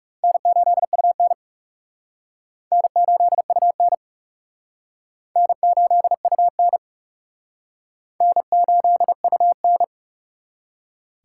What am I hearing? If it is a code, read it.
Morse code, heard as a keyed tone, about 36 wpm, 700 Hz. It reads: N8UN N8UN N8UN N8UN
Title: Speed Racing - USA 1x2 Call Signs 30wpm